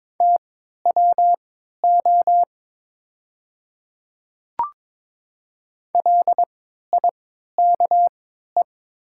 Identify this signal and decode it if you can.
Morse code, heard as a keyed tone, about 22 wpm, spaced as for 13 wpm, 700 Hz. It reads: TWO LIKE
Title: 3x Character Spacing - Top 100 Words - 22wpm